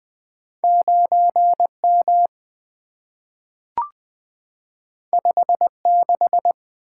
Morse code, heard as a keyed tone, about 20 wpm, 700 Hz. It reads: 9M 56
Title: Two Letter-Number Combinations - 20wpm